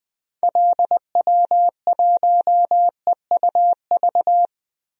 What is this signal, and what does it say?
Morse code, heard as a keyed tone, about 21 wpm, 700 Hz. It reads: LW1EUV
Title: DXCC-100 Call Signs - 6 Characters - 20wpm